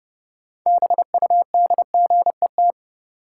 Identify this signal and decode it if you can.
Morse code, heard as a keyed tone, about 30 wpm, 700 Hz. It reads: BUDGET
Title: Sentences from Top 1200 Words - New Word Drill - 30wpm